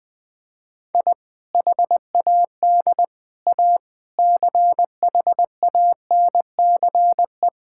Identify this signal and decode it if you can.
Morse code, heard as a keyed tone, about 20 wpm, 700 Hz. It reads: I HAD A CHANCE
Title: Sentences with 4-Word Phrases - 20wpm